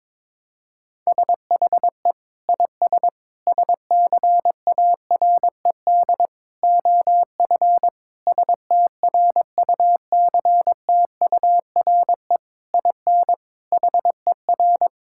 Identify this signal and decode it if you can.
Morse code, heard as a keyed tone, about 22 wpm, 700 Hz. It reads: SHE IS SCARED OF STRUCTURE IN HER
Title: Sentences from Top 1000 Words - Review Part 6 - 22wpm